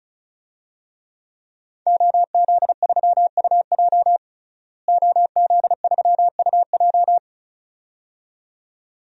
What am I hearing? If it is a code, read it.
Morse code, heard as a keyed tone, about 36 wpm, 700 Hz. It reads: OZ3UJ OZ3UJ
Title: DXCC-100 Call Signs - 5 Characters - Repeated 2x - 35wpm